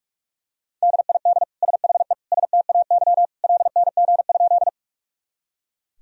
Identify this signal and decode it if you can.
Morse code, heard as a keyed tone, about 42 wpm, 700 Hz. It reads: DID SHE STAY LNG?
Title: Sentences from Top 500 Words and Abbreviations 45wpm